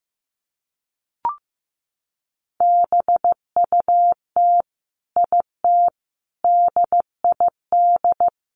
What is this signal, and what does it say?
Morse code, heard as a keyed tone, about 15 wpm, 700 Hz. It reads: BUT IT DID
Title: Sentences from Top 600 Words 15wpm